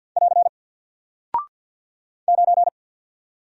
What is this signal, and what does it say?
Morse code, heard as a keyed tone, about 48 wpm, 700 Hz. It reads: <AR> 9
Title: Single Character (Letter:Number:Punctuation:Prosign) Alt 50wpm